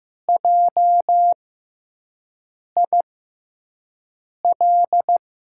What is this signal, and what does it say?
Morse code, heard as a keyed tone, about 15 wpm, spaced as for 5 wpm, 700 Hz. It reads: JIL
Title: USA Call Signs - 6x Character Spacing - 15wpm